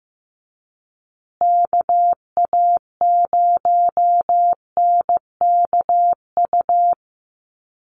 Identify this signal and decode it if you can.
Morse code, heard as a keyed tone, about 15 wpm, 700 Hz. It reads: KA0NKU
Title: USA Call Signs Repeated 3x 15wpm